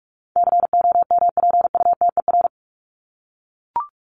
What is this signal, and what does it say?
Morse code, heard as a keyed tone, about 44 wpm, 700 Hz. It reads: COMPUTER
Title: Top 501-750 Words 45wpm